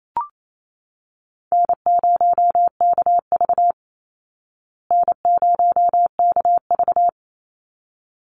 Morse code, heard as a keyed tone, about 28 wpm, 700 Hz. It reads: N0KV N0KV
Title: Call Signs Repeated 3x - 28wpm